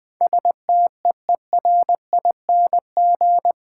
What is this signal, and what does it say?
Morse code, heard as a keyed tone, about 20 wpm, 690 Hz. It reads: STEERING
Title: Easy Words with Hints - 20wpm